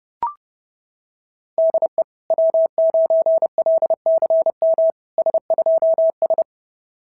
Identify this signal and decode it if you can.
Morse code, heard as a keyed tone, about 31 wpm, 690 Hz. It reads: DE W9LCM S2S
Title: SOTA QSO Practice 30wpm